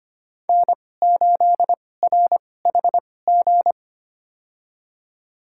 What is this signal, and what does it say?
Morse code, heard as a keyed tone, about 25 wpm, spaced as for 18 wpm, 700 Hz. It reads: N8RHG
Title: USA Call Signs - 5 Characters - 2x Character Spacing - 25wpm